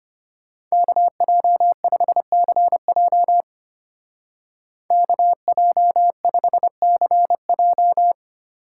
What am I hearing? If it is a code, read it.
Morse code, heard as a keyed tone, about 30 wpm, 700 Hz. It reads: KJ5CJ KJ5CJ
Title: Speed Racing - USA Call Signs 25wpm